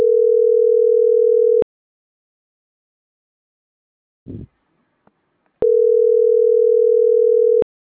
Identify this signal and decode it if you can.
Ringback tone (North America), 440 + 480 Hz.